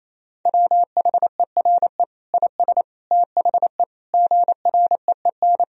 Morse code, heard as a keyed tone, about 28 wpm, 690 Hz. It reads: WHERE IS THE GREEN
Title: Sentences from Top 1000 Words - Review Part 2 - 28wpm